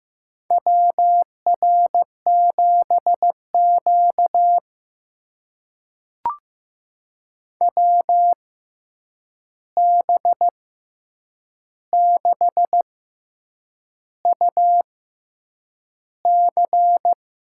Morse code, heard as a keyed tone, about 15 wpm, spaced as for 7 wpm, 700 Hz. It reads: WR7Q WB6UC
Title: USA Call Signs - 6x Character Spacing - 15wpm